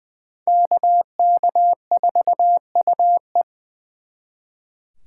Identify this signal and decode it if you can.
Morse code, heard as a keyed tone, about 26 wpm, 700 Hz. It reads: KK4UE